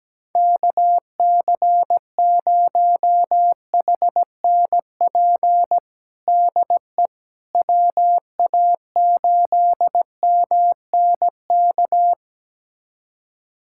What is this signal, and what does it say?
Morse code, heard as a keyed tone, about 17 wpm, 700 Hz. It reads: KC0HNP DE WA8MNK
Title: USA Callsign DE Callsign 17wpm